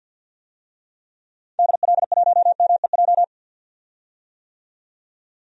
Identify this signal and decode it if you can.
Morse code, heard as a keyed tone, about 50 wpm, 700 Hz. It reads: DL1MEJ
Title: DXCC-100 Call Signs - 6 Characters - 50wpm